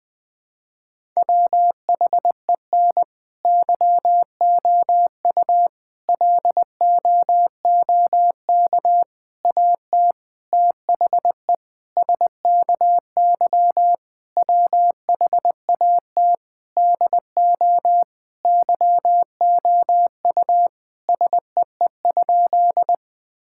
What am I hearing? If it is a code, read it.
Morse code, heard as a keyed tone, about 20 wpm, 700 Hz. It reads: WHEN YOU LOOK AT THE SKY WHAT DO YOU SEE?
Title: Sentences with 3-Word Phrases - Spoken Beginning - 20wpm